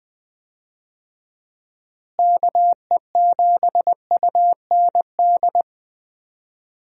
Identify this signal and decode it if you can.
Morse code, heard as a keyed tone, about 20 wpm, 700 Hz. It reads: KE7UND
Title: USA Call Signs - 6 Characters - Repeated 3x - 20wpm